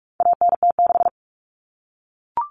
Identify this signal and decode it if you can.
Morse code, heard as a keyed tone, about 44 wpm, 700 Hz. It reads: ANT6
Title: Common Cut Numbers - 4-Digit 45wpm